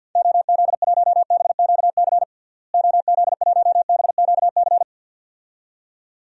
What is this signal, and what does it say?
Morse code, heard as a keyed tone, about 48 wpm, 700 Hz. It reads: OZ1BQC OZ1BQC
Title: DXCC-100 Call Signs - 6 Characters - Repeated 2x - 50wpm